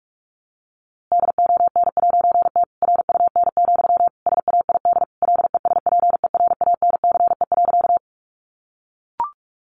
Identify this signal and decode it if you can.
Morse code, heard as a keyed tone, about 46 wpm, 700 Hz. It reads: DON'T RUN, SAID LESPERANCE.
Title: A Sound Of Thunder 45wpm